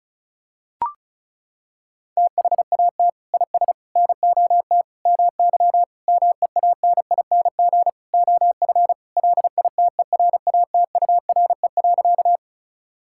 Morse code, heard as a keyed tone, about 36 wpm, 700 Hz. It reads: THAT IS NOT MY MEANING OF LITERATURE.
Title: Sentences from Top 1000 Words - Review Part 6 - 35wpm